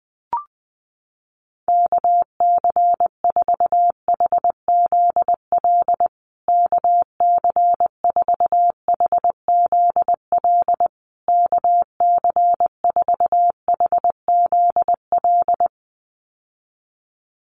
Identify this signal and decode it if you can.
Morse code, heard as a keyed tone, about 20 wpm, 700 Hz. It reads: KC4HZL KC4HZL KC4HZL
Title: USA Call Signs Repeated 3x 20wpm